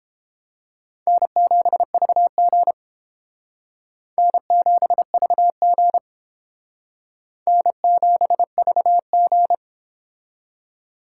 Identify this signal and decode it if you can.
Morse code, heard as a keyed tone, about 30 wpm, 700 Hz. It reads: N7VG N7VG N7VG